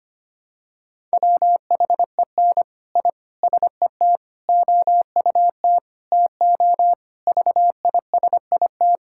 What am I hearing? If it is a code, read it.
Morse code, heard as a keyed tone, about 25 wpm, 700 Hz. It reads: WHEN I SET OUT TO VISIT
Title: Sentences from Top 1000 Words - Review Part 6 - 25wpm